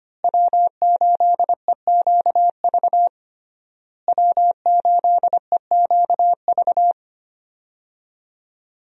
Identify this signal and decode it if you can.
Morse code, heard as a keyed tone, about 25 wpm, 700 Hz. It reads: W8EQV W8EQV
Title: USA Call Signs - 5 Characters - Repeated 3x - 25wpm